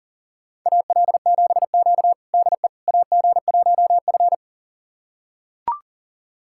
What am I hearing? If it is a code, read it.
Morse code, heard as a keyed tone, about 40 wpm, 700 Hz. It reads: AL7Q DE AG1F
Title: USA Callsign DE Callsign - 4 Characters - 40wpm